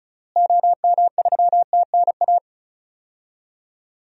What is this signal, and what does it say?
Morse code, heard as a keyed tone, about 36 wpm, 700 Hz. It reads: OM3TNA
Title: DXCC-100 Call Signs - 6 Characters - Repeated 3x - 35wpm